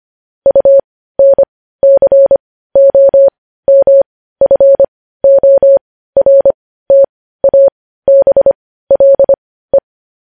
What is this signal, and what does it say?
Morse code, heard as a keyed tone, about 25 wpm, spaced as for 15 wpm, 550 Hz. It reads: UNCOMFORTABLE
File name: podcast / News headlines-1x-25x15